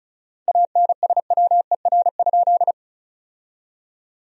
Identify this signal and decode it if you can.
Morse code, heard as a keyed tone, about 34 wpm, 700 Hz. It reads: ANSWER?
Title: 4x Word-Spacing - Sentences from Top 300 Words - 35wpm